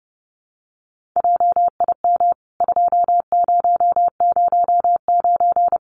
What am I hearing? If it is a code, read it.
Morse code, heard as a keyed tone, about 30 wpm, 700 Hz. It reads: JIM 2009
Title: CWT Exchange 30wpm